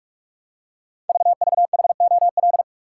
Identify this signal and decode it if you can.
Morse code, heard as a keyed tone, about 44 wpm, 700 Hz. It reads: VVHOL
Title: Code Groups - 5 Letters - 45wpm